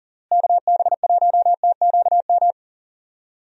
Morse code, heard as a keyed tone, about 40 wpm, 700 Hz. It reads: KB1TQM
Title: USA Call Signs - 6 Characters - Repeated 3x - 40wpm